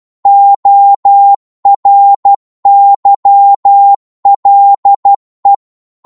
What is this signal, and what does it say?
Morse code, heard as a keyed tone, about 12 wpm, 800 Hz. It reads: ORYLE
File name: Morse / ltr-12wpm